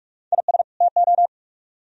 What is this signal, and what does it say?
Morse code, heard as a keyed tone, about 44 wpm, 700 Hz. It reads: IS TO